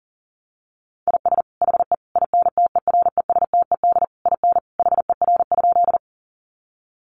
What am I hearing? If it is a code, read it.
Morse code, heard as a keyed tone, about 40 wpm, 700 Hz. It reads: IS HE INTERESTED IN HER?
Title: Sentences from Top 900 Words 40wpm